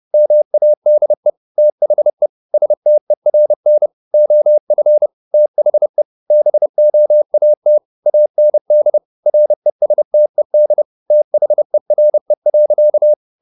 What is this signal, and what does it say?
Morse code, heard as a keyed tone, about 30 wpm, 600 Hz. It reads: MADE THE STERN OF THE BOAT AND RESTED THERE.
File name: the old man and the sea / the old man and the sea 1 30wpm